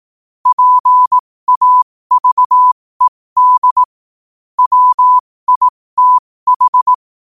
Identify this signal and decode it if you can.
Morse code, heard as a keyed tone, about 18 wpm, spaced as for 15 wpm, 1000 Hz. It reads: PAVED WITH
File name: qotdfiles / 15wpm